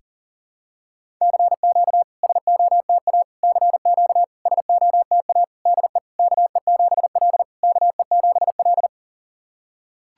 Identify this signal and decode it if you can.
Morse code, heard as a keyed tone, about 40 wpm, 690 Hz. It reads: CQ SOTA CQ SOTA DE KE7L KE7L
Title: SOTA QSO Practice 40wpm